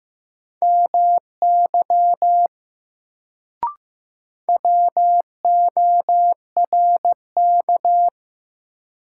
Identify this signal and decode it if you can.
Morse code, heard as a keyed tone, about 15 wpm, 700 Hz. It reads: MY WORK
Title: Sentences from Top 200 Words - New Word Drill - 15wpm